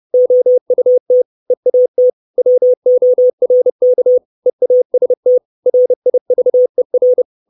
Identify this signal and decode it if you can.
Morse code, heard as a keyed tone, about 30 wpm, 500 Hz. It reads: OUT EAT WORK EAST RIVER
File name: 632 Words 30WPM